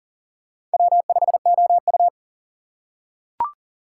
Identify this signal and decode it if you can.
Morse code, heard as a keyed tone, about 38 wpm, 700 Hz. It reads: W5OU
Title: USA Call Signs - 4 Characters - 4x Character Spacing - 40wpm